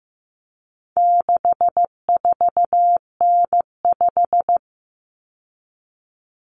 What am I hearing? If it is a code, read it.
Morse code, heard as a keyed tone, about 15 wpm, 700 Hz. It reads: 64N5
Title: Common Cut Numbers - 4-Digit 15wpm